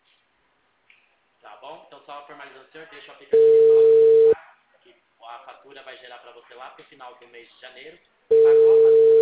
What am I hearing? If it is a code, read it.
Ringback tone (China, 450 Hz), with speech.